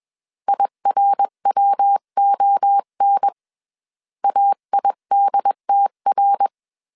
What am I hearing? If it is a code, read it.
Morse code, heard as a keyed tone, about 21 wpm, 800 Hz. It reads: IRWON AIDTR
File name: 20wpm